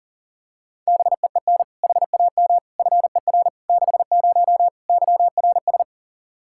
Morse code, heard as a keyed tone, about 40 wpm, 700 Hz. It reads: BEEN HAM FER 60 YRS
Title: Practice QSO - Exchange 3 - More Info - 40wpm